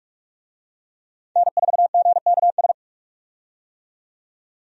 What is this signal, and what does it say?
Morse code, heard as a keyed tone, about 46 wpm, 700 Hz. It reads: N4GKS
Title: USA Call Signs - 5 Characters - 45wpm